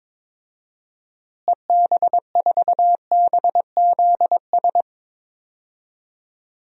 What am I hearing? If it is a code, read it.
Morse code, heard as a keyed tone, about 22 wpm, 700 Hz. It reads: EB4BZS
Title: DXCC-100 Call Signs - 6 Characters - Repeated 3x - 22wpm